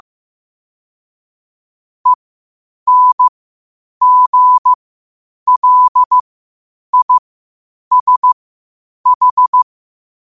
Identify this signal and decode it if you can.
Morse code, heard as a keyed tone, about 15 wpm, spaced as for 8 wpm, 1000 Hz. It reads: ENGLISH
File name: qotdfiles / 8wpm